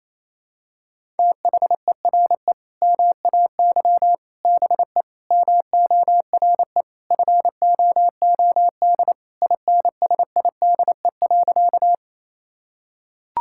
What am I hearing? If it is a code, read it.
Morse code, heard as a keyed tone, about 28 wpm, 700 Hz. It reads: THERE MAY BE MORE FOOD INSIDE.
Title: Sentences from Top 1000 Words - Review Part 6 - 28wpm